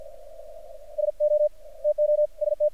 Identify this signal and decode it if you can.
Morse code, heard as a keyed tone, about 50 wpm, 600 Hz. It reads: SO TO ST
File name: archived / CRSnet